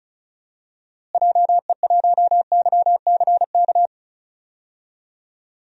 Morse code, heard as a keyed tone, about 36 wpm, 700 Hz. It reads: JE1YCK
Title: DXCC-100 Call Signs - 6 Characters - Repeated 3x - 35wpm